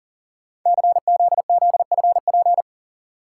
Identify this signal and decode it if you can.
Morse code, heard as a keyed tone, about 40 wpm, 700 Hz. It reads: CZZFP
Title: Code Groups - 5 Letter:Numbers - 40wpm